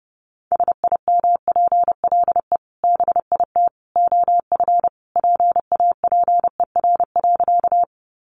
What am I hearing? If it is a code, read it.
Morse code, heard as a keyed tone, about 29 wpm, 700 Hz. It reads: SIMPLE BIT OF PAPER.
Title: Sentences from Top 900 Words 30wpm